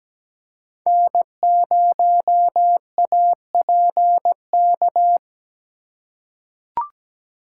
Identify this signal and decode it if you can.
Morse code, heard as a keyed tone, about 17 wpm, 700 Hz. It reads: N0APK
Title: USA Call Signs - 5 Characters - 2x Character Spacing - 17wpm